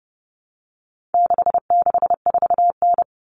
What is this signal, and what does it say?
Morse code, heard as a keyed tone, about 30 wpm, 700 Hz. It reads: 664N
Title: Common Cut Numbers - 4-Digit 30wpm